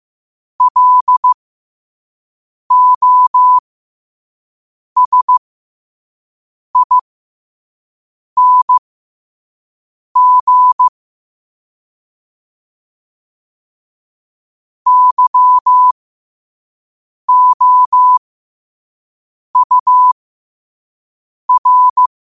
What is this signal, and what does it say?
Morse code, heard as a keyed tone, about 15 wpm, spaced as for 5 wpm, 1000 Hz. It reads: LOSING YOUR
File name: qotdfiles / 5wpm